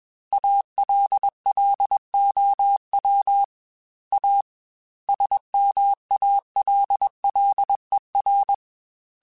Morse code, heard as a keyed tone, about 21 wpm, 800 Hz. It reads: ALLOW A SMALLER
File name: Technical Subjects / Technical Subjects-mp3s / Sw Pwr Sup @20